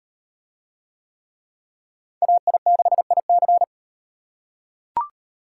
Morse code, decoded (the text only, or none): AI6IC